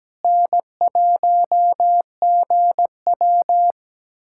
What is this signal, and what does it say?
Morse code, heard as a keyed tone, about 17 wpm, 700 Hz. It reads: N1GW
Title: USA Callsign DE Callsign - 4 Characters - 17wpm